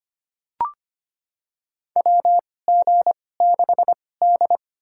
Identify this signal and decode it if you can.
Morse code, heard as a keyed tone, about 25 wpm, spaced as for 18 wpm, 700 Hz. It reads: WG6D
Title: USA Call Signs - 2x Character Spacing - 25wpm